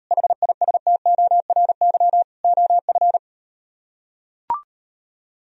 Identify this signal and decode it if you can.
Morse code, heard as a keyed tone, about 38 wpm, 700 Hz. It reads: HISTORY OF